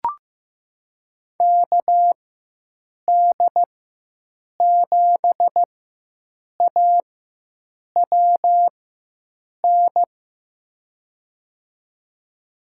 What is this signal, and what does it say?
Morse code, heard as a keyed tone, about 16 wpm, spaced as for 7 wpm, 700 Hz. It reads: KD7AWN
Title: USA Call Signs - 6 Characters - 4x Character Spacing - 15wpm